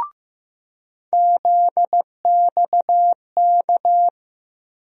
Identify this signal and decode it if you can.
Morse code, heard as a keyed tone, about 16 wpm, 700 Hz. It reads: ZXK
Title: USA Call Sign Postfixes - 3 Letters 15wpm